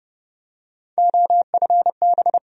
Morse code, heard as a keyed tone, about 30 wpm, 700 Hz. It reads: OFB
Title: USA Call Sign Postfixes - 3 Letters 30wpm